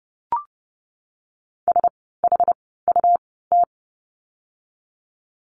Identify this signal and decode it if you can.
Morse code, heard as a keyed tone, about 30 wpm, spaced as for 17 wpm, 700 Hz. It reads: SHUT
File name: Lesson 17f - Letter u - Words 30wpm